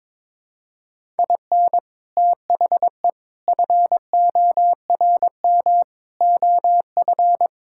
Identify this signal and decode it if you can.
Morse code, heard as a keyed tone, about 22 wpm, 700 Hz. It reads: IN THE FORM OF
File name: practice / 22wpm / Sentences with 4-Word Phrases - 22wpm